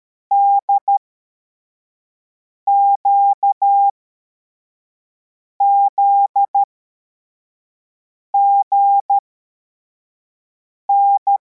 Morse code, heard as a keyed tone, about 13 wpm, spaced as for 4 wpm, 800 Hz. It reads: DQZGN